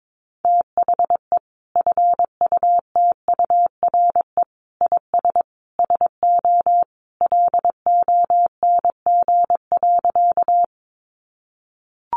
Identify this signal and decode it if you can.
Morse code, heard as a keyed tone, about 22 wpm, 700 Hz. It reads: THE FUTURE IS SO LONG.
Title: Sentences from Top 700 Words 22wpm